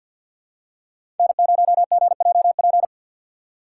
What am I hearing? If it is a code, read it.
Morse code, heard as a keyed tone, about 53 wpm, 700 Hz. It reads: N0GJP
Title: USA Call Signs - 5 Characters - 6x Character Spacing - 50wpm